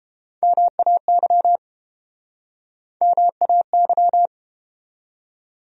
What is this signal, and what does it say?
Morse code, heard as a keyed tone, about 33 wpm, 700 Hz. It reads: MAY MAY